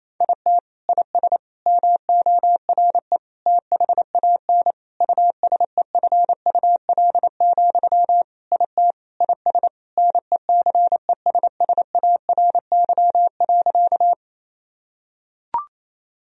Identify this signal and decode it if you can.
Morse code, heard as a keyed tone, about 28 wpm, 700 Hz. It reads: IT IS MORE THAN USEFUL, IT IS NECESSARY.
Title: Sentences from Top 1000 Words - Review Part 4 - 28wpm